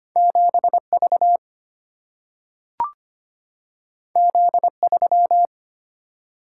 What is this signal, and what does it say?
Morse code, heard as a keyed tone, about 25 wpm, 700 Hz. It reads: 7V Z3
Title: Single-Letter Single-Number 25wpm